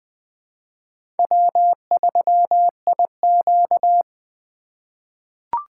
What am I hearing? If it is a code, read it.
Morse code, heard as a keyed tone, about 20 wpm, 700 Hz. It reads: W3IQ
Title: USA Call Signs - 4 Characters - 3x Character Spacing - 20wpm